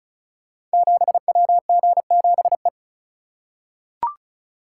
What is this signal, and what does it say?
Morse code, heard as a keyed tone, about 36 wpm, 700 Hz. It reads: 7WG7E
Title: Code Groups - 5 Letter:Numbers - 35wpm